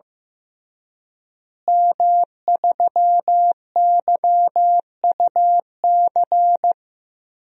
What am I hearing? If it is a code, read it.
Morse code, heard as a keyed tone, about 15 wpm, 700 Hz. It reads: M3YUC
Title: DXCC-100 Call Signs - 5 Characters - Repeated 3x - 15wpm